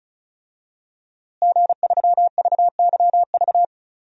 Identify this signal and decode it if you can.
Morse code, heard as a keyed tone, about 34 wpm, 700 Hz. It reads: G3VYV
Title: DXCC-100 Call Signs - 5 Characters - Repeated 3x - 35wpm